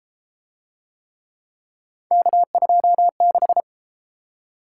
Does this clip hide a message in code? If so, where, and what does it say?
Yes: Morse code from 2.1 to 3.6 s, heard as a keyed tone, about 29 wpm, 700 Hz. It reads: K26